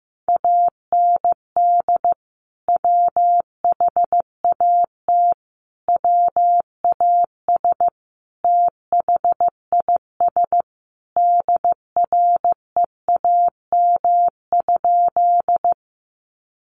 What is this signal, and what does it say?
Morse code, heard as a keyed tone, about 15 wpm, 700 Hz. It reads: AND WHAT WAS THIS DREAM?
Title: Sentences from Top 1500 Words 15wpm